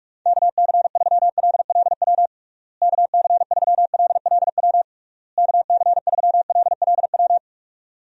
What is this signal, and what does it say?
Morse code, heard as a keyed tone, about 45 wpm, 700 Hz. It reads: KC3LLW KC3LLW KC3LLW
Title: USA Call Signs - 6 Characters - Repeated 3x - 45wpm